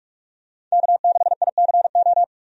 Morse code, heard as a keyed tone, about 43 wpm, 700 Hz. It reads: K6ICO
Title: USA Call Signs - 5 Characters - 45wpm